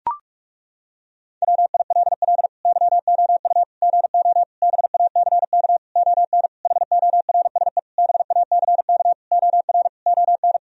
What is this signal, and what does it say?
Morse code, heard as a keyed tone, about 41 wpm, 700 Hz. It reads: WILL YOU GO BACK ON HORSE BACK OR ON
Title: Sentences from Top 1000 Words - Review Part 3 - 45wpm